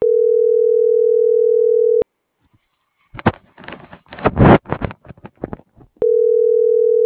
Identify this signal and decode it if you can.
Ringback tone (North America), 440 + 480 Hz.